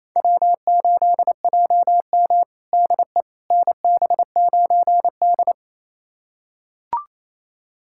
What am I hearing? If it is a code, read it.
Morse code, heard as a keyed tone, about 28 wpm, 700 Hz. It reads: W8JM DE NB9D
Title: USA Callsign DE Callsign - 4 Characters - 28wpm